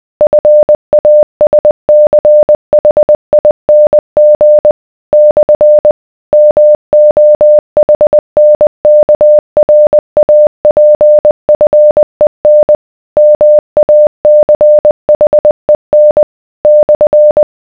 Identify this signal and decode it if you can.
Morse code, heard as a keyed tone, about 20 wpm, 600 Hz. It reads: FASCHING / MOHNKRAPFEN MACHEN /